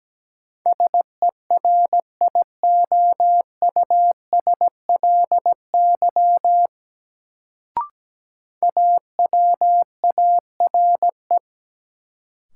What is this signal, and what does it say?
Morse code, heard as a keyed tone, about 17 wpm, 700 Hz. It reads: SERIOUSLY AWARE
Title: Sentences from Top 1200 Words - New Word Drill - 17wpm